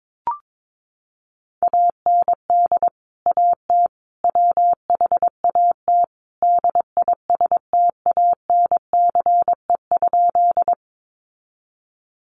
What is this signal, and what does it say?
Morse code, heard as a keyed tone, about 22 wpm, 700 Hz. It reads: AND AT WHAT DISTANCE?
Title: Sentences from Top 1200 Words 22wpm